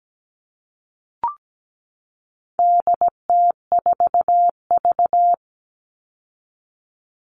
Morse code, heard as a keyed tone, about 17 wpm, 700 Hz. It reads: DT4V
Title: All Cut Numbers - 4-Digit 17wpm